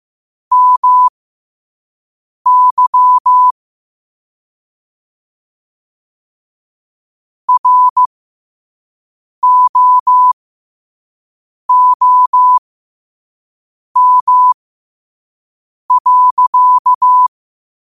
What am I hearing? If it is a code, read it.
Morse code, heard as a keyed tone, about 15 wpm, spaced as for 5 wpm, 1000 Hz. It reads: MY ROOM.
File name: qotdfiles / 5wpm